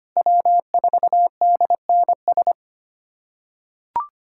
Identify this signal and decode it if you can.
Morse code, heard as a keyed tone, about 25 wpm, 700 Hz. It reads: W4DNS
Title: USA Callsign DE Callsign - 5 Characters - 25wpm